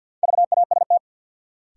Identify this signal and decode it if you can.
Morse code, heard as a keyed tone, about 53 wpm, 700 Hz. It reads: VAST
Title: Sentences from Top 1300 Words - New Word Drill Alt - 50wpm